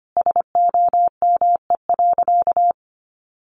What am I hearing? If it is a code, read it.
Morse code, heard as a keyed tone, about 25 wpm, 700 Hz. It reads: SOME.
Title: Sentences from Top 800 Words 25wpm